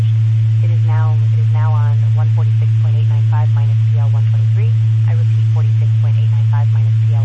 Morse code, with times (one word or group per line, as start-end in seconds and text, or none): none